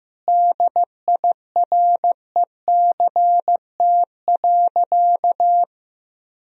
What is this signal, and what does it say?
Morse code, heard as a keyed tone, about 15 wpm, 700 Hz. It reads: DIRECT.
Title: Sentences from Top 1000 Words - Review Part 2 - 15wpm